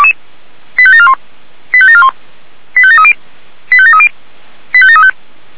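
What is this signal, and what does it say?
Five-tone selective call (ZVEI), heard as quick runs of five tones, tones of about 70 ms. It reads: 85741 85741 85730 85730 85735
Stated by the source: ZVEI five-tone selective call received on SDR.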